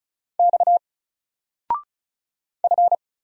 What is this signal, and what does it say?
Morse code, heard as a keyed tone, about 36 wpm, 700 Hz. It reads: X F
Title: Single Character (Letter:Number:Punctuation:Prosign) 35wpm